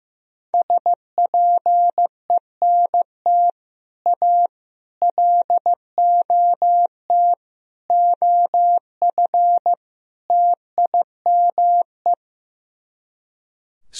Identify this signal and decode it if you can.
Morse code, heard as a keyed tone, about 16 wpm, 700 Hz. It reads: SPENT A LOT OF TIME
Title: Speed-Racing - Sets of 5 Words 15wpm